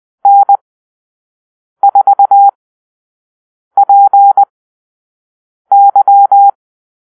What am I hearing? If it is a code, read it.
Morse code, heard as a keyed tone, about 20 wpm, spaced as for 6 wpm, 800 Hz. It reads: N4PY
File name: arrl / Call-Sign Drill-5